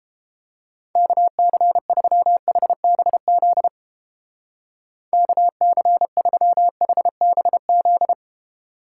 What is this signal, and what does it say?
Morse code, heard as a keyed tone, about 29 wpm, 700 Hz. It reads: KC3HBZ KC3HBZ